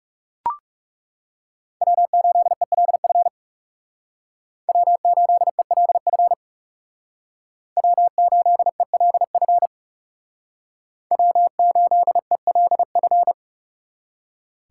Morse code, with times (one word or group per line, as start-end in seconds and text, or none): 1.8-3.3 W8ELF
4.7-6.3 W8ELF
7.8-9.7 W8ELF
11.1-13.3 W8ELF